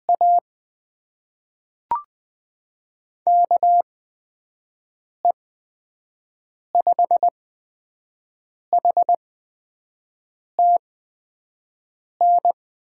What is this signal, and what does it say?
Morse code, heard as a keyed tone, about 20 wpm, spaced as for 5 wpm, 700 Hz. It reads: A KE5HTN
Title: USA Call Signs - 8x Character Spacing - 20wpm